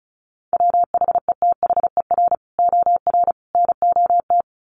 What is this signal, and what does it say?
Morse code, heard as a keyed tone, about 34 wpm, 700 Hz. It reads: WHETHER OR NOT
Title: Sets of 3 Words 35wpm